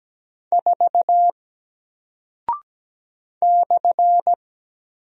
Morse code, heard as a keyed tone, about 17 wpm, 700 Hz. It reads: 4 /
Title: Single Character (Letter:Number:Punctuation:Prosign) Alt 17wpm